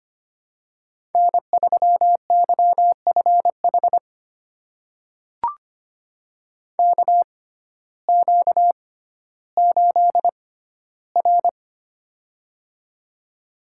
Morse code, heard as a keyed tone, about 25 wpm, spaced as for 14 wpm, 700 Hz. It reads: N3YFH KQ8R
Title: USA Call Signs - 6x Character Spacing - 25wpm